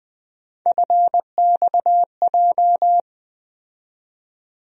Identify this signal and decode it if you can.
Morse code, heard as a keyed tone, about 21 wpm, 700 Hz. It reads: FXJ
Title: USA Call Sign Postfixes - 3 Letters 20wpm